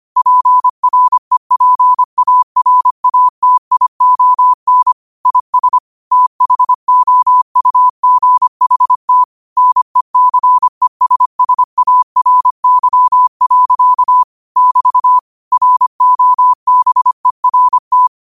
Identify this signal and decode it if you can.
Morse code, heard as a keyed tone, about 25 wpm, 1000 Hz. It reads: PREPARATION IS THOUGHT NECESSARY. <BT> ROBERT
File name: qotdfiles / 25wpm